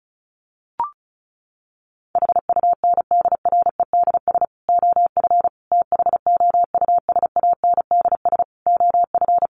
Morse code, heard as a keyed tone, about 34 wpm, 700 Hz. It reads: HUNDREDS OF THOUSANDS OF
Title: Sets of 5 Words Encore 35wpm